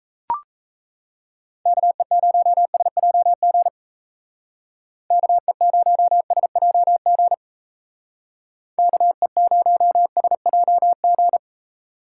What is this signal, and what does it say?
Morse code, heard as a keyed tone, about 34 wpm, 700 Hz. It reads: KE0SJG KE0SJG KE0SJG
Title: Speed Racing - USA Call Signs 28wpm